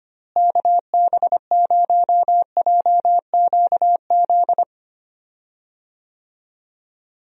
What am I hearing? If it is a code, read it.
Morse code, heard as a keyed tone, about 25 wpm, 700 Hz. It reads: KB0JQZ
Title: USA Call Signs - 6 Characters - Repeated 3x - 25wpm